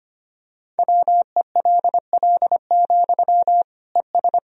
Morse code, heard as a keyed tone, about 25 wpm, 700 Hz. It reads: WELL, ES
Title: Sentences from Top 200 Words and Abbreviations 25wpm